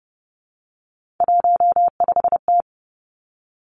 Morse code, heard as a keyed tone, about 30 wpm, 700 Hz. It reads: 15T
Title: Three Challenging Letter-Number Combinations 30wpm